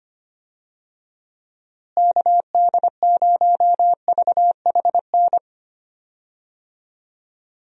Morse code, heard as a keyed tone, about 25 wpm, 700 Hz. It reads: KD0VHN
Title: USA Call Signs - 6 Characters - Repeated 2x - 25wpm